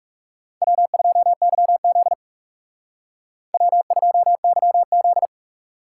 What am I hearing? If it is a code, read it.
Morse code, heard as a keyed tone, about 39 wpm, 700 Hz. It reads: W2YZ W2YZ